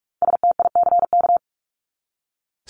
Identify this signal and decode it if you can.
Morse code, heard as a keyed tone, about 44 wpm, 700 Hz. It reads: STICK
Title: Top 1001-1250 Words 45wpm